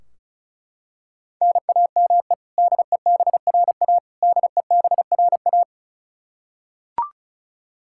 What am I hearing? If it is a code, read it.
Morse code, heard as a keyed tone, about 34 wpm, 700 Hz. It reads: NAME DEBRA DEBRA
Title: Practice QSO - Exchange 1 - Beginning - 35wpm